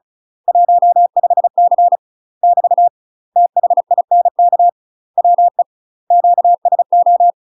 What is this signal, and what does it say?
Morse code, heard as a keyed tone, about 34 wpm, 700 Hz. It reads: 15C <BT> THINK WE QSO
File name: UK MASTER QSOs TRAINING / 35WPM / UK MASTER QSOs 0004